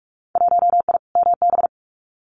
Morse code, heard as a keyed tone, about 46 wpm, 700 Hz. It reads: 1I MB
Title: Winter Field Day Exchanges Encore 45wpm